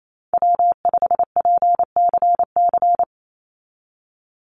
Morse code, heard as a keyed tone, about 28 wpm, 700 Hz. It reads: W5PCC
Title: USA Call Signs 28wpm